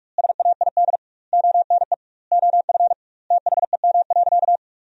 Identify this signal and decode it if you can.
Morse code, heard as a keyed tone, about 47 wpm, 700 Hz. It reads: SAID ONE OF THEM.
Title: 2x Word-Spacing - Sentences from Top 300 Words - 45wpm